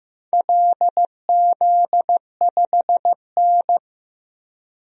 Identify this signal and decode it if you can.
Morse code, heard as a keyed tone, about 15 wpm, 700 Hz. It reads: LZ5N
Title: DXCC-100 Call Signs - 4 Characters - Repeated 2x - 15wpm